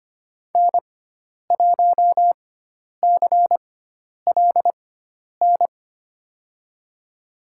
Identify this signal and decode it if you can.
Morse code, heard as a keyed tone, about 25 wpm, spaced as for 10 wpm, 700 Hz. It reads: N1CLN
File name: Lesson 16g - Letter c - Call Signs 25wpm